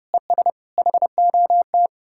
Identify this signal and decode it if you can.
Morse code, heard as a keyed tone, about 40 wpm, 700 Hz. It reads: ES HOT